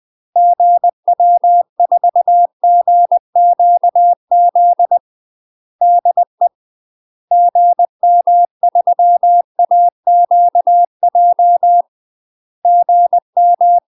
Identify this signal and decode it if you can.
Morse code, heard as a keyed tone, about 20 wpm, 700 Hz. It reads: GW4GQZ DE GM3AQJ GM
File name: UK MASTER QSOs TRAINING / 20WPM / UK MASTER QSOs 0001